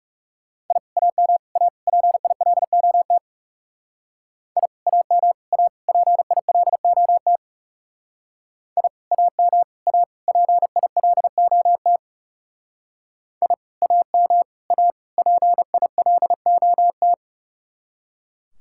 Morse code, heard as a keyed tone, about 36 wpm, 700 Hz. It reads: I AM A PILOT I AM A PILOT I AM A PILOT I AM A PILOT
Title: Speed-Racing Common QSO Phrases - 30wpm